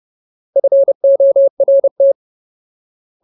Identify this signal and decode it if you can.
Morse code, heard as a keyed tone, about 30 wpm, 550 Hz. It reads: FORT
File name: Ultimate Word List / 30WPM Ultimate